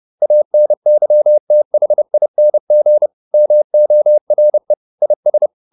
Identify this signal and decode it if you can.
Morse code, heard as a keyed tone, about 30 wpm, 600 Hz. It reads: ANYTHING MORE IS